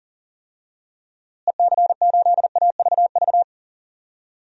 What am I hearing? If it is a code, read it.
Morse code, heard as a keyed tone, about 40 wpm, 700 Hz. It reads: EC8AVV
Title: DXCC-100 Call Signs - 6 Characters - Repeated 3x - 40wpm